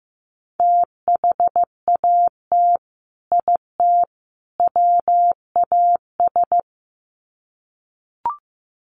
Morse code, heard as a keyed tone, about 15 wpm, 700 Hz. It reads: THAT IT WAS
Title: Sets of 3 Words 15wpm